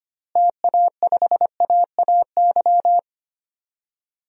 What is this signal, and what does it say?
Morse code, heard as a keyed tone, about 25 wpm, 700 Hz. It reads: TA5AAY
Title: DXCC-100 Call Signs - 6 Characters - Repeated 2x - 25wpm